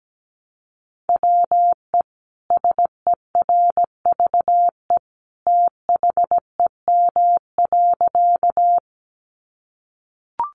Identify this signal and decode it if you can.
Morse code, heard as a keyed tone, about 17 wpm, 700 Hz. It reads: WE SERVE THEM.
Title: Sentences from Top 400 Words 17wpm